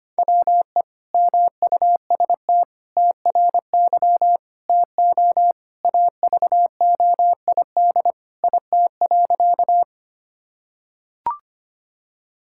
Morse code, heard as a keyed tone, about 25 wpm, 700 Hz. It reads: WE MUST TRY TO AVOID IT.
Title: Sentences from Top 1000 Words - Review Part 6 - 25wpm